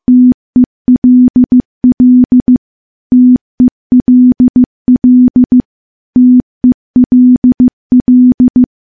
Morse code, heard as a keyed tone, about 15 wpm, 250 Hz. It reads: TELL TELL TELL
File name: morse / morse-training-15WPM